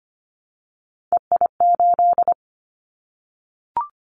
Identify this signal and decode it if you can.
Morse code, heard as a keyed tone, about 25 wpm, 700 Hz. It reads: EI8
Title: Callsign Prefixes for Top 50 DXCC - 25wpm